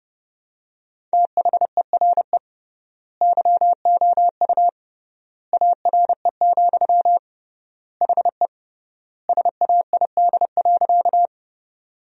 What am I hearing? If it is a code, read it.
Morse code, heard as a keyed tone, about 30 wpm, 700 Hz. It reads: THERE YOU ARE, HE SAID.
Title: 3x Word-Spacing - Sentences from Top 100 Words - 30wpm